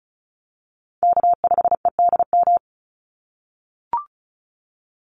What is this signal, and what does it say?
Morse code, heard as a keyed tone, about 34 wpm, 700 Hz. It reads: K5EDM
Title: Call Signs - 35wpm